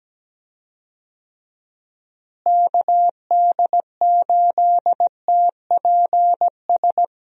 Morse code, heard as a keyed tone, about 17 wpm, 700 Hz. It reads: KD8TPS
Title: USA Call Signs - 6 Characters - Repeated 2x - 17wpm